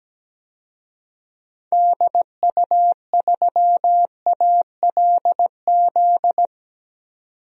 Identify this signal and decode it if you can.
Morse code, heard as a keyed tone, about 17 wpm, 700 Hz. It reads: DU3ALZ
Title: DXCC-100 Call Signs - 6 Characters - 17wpm